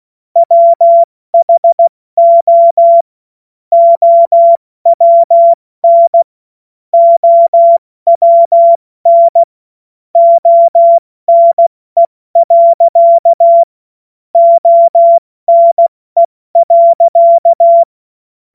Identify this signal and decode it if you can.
Morse code, heard as a keyed tone, about 16 wpm, 680 Hz. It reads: WHO OWN OWN ONE. ONE.